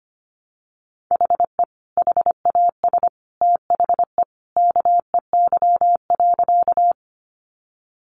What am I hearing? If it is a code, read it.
Morse code, heard as a keyed tone, about 25 wpm, 700 Hz. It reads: HE HAS THE KEY.
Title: Sentences from Top 600 Words 25wpm